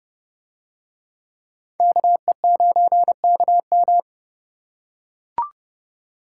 Morse code, heard as a keyed tone, about 29 wpm, 700 Hz. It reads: KE9KM